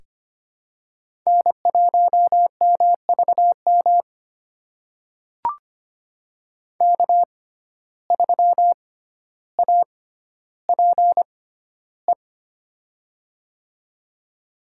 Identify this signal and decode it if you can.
Morse code, heard as a keyed tone, about 25 wpm, spaced as for 13 wpm, 700 Hz. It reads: N1MVM K3APE
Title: USA Call Signs - 5 Characters - 6x Character Spacing - 25wpm